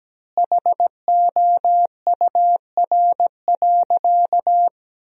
Morse code, heard as a keyed tone, about 17 wpm, 700 Hz. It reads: HOUR.
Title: 4x Word-Spacing - Sentences from Top 400 Words - 17wpm